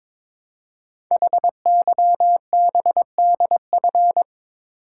Morse code, heard as a keyed tone, about 22 wpm, 700 Hz. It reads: HYBDF
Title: Code Groups - 5 Letter:Numbers - 22wpm